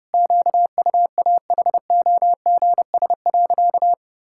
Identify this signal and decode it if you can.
Morse code, heard as a keyed tone, about 31 wpm, 700 Hz. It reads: QUAHOGS.
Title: Pangram Challenge v2 - 30wpm